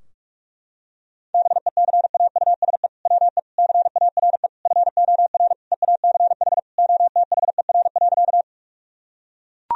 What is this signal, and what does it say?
Morse code, heard as a keyed tone, about 41 wpm, 700 Hz. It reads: BECAUSE WE CARE FOR EACH OTHER.
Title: Sentences from Top 1000 Words - Review Part 3 - 45wpm